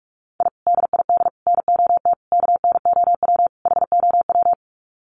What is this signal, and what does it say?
Morse code, heard as a keyed tone, about 44 wpm, 700 Hz. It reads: I DID NOT KNOW HOW
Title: Sets of 5 Words Encore 45wpm